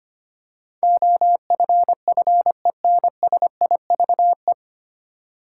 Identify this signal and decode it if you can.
Morse code, heard as a keyed tone, about 25 wpm, 700 Hz. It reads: OFFENSIVE
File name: practice / 25wpm / 9 Letter Words - First 3 Letters Given - 25wpm